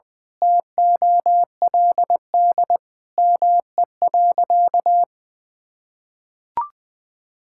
Morse code, heard as a keyed tone, about 21 wpm, 700 Hz. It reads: TOLD ME.
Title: Sentences from Top 1000 Words - Review Part 6 - 20wpm